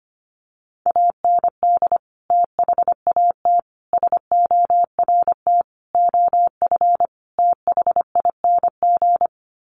Morse code, heard as a keyed tone, about 25 wpm, 700 Hz. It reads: AND THAT SORT OF THING
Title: Sets of 5 Words Encore2 25wpm